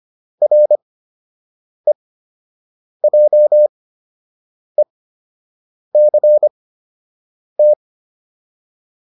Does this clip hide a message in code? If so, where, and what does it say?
Yes: Morse code from 0.4 to 7.7 s, heard as a keyed tone, about 25 wpm, spaced as for 7 wpm, 600 Hz. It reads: REJECT